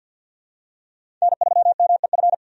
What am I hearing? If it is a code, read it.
Morse code, heard as a keyed tone, about 52 wpm, 700 Hz. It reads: N3MEF
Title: USA Call Signs - 5 Characters - 50wpm